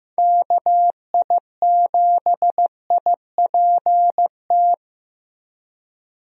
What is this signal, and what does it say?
Morse code, heard as a keyed tone, about 15 wpm, 700 Hz. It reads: KI7IPT
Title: USA Call Signs - 6 Characters - 15wpm